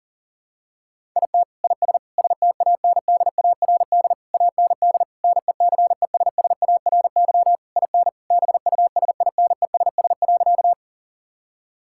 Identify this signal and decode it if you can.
Morse code, heard as a keyed tone, about 40 wpm, 700 Hz. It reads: IT IS STANDARD AND NECESSARY IN BUSINESS.
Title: Sentences from Top 1000 Words - Review Part 6 - 40wpm